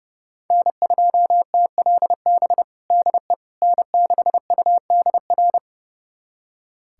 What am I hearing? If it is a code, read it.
Morse code, heard as a keyed tone, about 30 wpm, 700 Hz. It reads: N2TLB DE N6UDR
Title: USA Callsign DE Callsign - 5 Characters - 30wpm